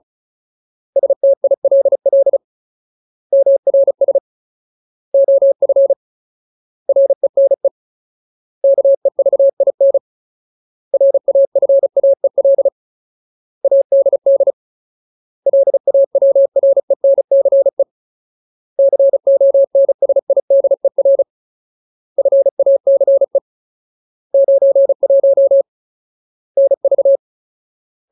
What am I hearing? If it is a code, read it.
Morse code, heard as a keyed tone, about 34 wpm, 550 Hz. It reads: STILL MRS OF RENE KEVIN RAFAEL ADD LAWRENCE CONSIDER FACE 91 NV
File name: Ultimate Word List / 35WPM Ultimate